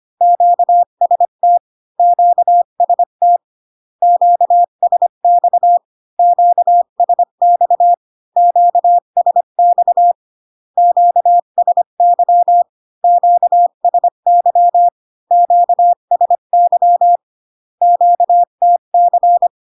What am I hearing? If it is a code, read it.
Morse code, heard as a keyed tone, about 25 wpm, 700 Hz. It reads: QST QST QSX QSX QSX QSY QSY QSY QTC